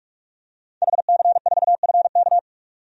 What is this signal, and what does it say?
Morse code, heard as a keyed tone, about 46 wpm, 700 Hz. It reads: HC4FK
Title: DXCC-100 Call Signs - 5 Characters - 45wpm